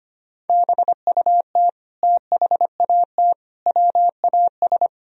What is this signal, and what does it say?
Morse code, heard as a keyed tone, about 25 wpm, 700 Hz. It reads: BUT THAT WAS
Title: Sentences from Top 1000 Words - Review Part 1 - 25wpm